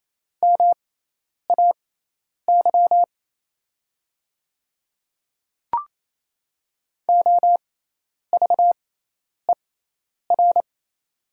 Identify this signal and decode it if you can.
Morse code, heard as a keyed tone, about 28 wpm, spaced as for 10 wpm, 700 Hz. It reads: MAY OVER
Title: 6x Character Spacing - Top 100 Words - 28wpm